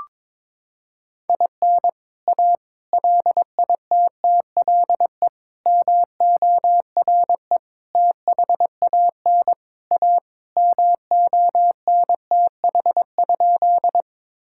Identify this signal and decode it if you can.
Morse code, heard as a keyed tone, about 22 wpm, 700 Hz. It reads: IN A LITTLE MORE THAN A MONTH?
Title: Sentences from Top 1000 Words - Review Part 5 - 22wpm